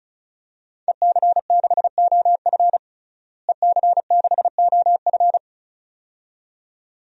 Morse code, heard as a keyed tone, about 34 wpm, 700 Hz. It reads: EC6OF EC6OF
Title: DXCC-100 Call Signs - 5 Characters - Repeated 2x - 35wpm